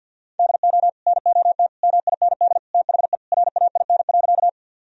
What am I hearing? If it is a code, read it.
Morse code, heard as a keyed tone, about 50 wpm, 700 Hz. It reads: DO NOT MIND THE RAIN.
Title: Sentences from Top 1000 Words - Review Part 1 - 50wpm